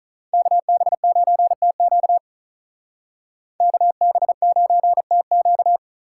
Morse code, heard as a keyed tone, about 44 wpm, 700 Hz. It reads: KB9TQ KB9TQ